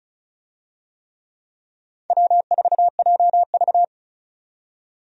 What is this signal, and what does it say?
Morse code, heard as a keyed tone, about 34 wpm, 700 Hz. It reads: W4JV
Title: USA Call Signs - 4 Characters - 35wpm